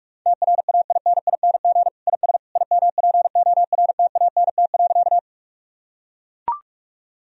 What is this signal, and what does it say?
Morse code, heard as a keyed tone, about 41 wpm, 700 Hz. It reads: TRAINING IS IMPORTANT.
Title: Sentences from Top 1000 Words - Review Part 4 - 45wpm